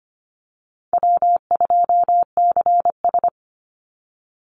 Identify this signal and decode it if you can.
Morse code, heard as a keyed tone, about 25 wpm, 700 Hz. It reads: W2CS
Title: Easy Call Signs Repeated 2x - 25wpm